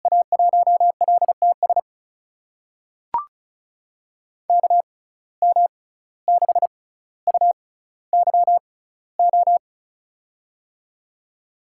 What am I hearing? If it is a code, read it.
Morse code, heard as a keyed tone, about 36 wpm, spaced as for 17 wpm, 690 Hz. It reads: A1LTS KM6UYO
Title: USA Call Signs - 6 Characters - 6x Character Spacing - 35wpm